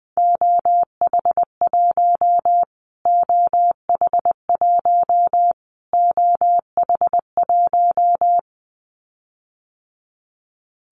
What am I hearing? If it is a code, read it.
Morse code, heard as a keyed tone, about 20 wpm, 700 Hz. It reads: OH1 OH1 OH1
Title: Callsign Prefixes for Top 25 DXCC - Repeated 3x - 20wpm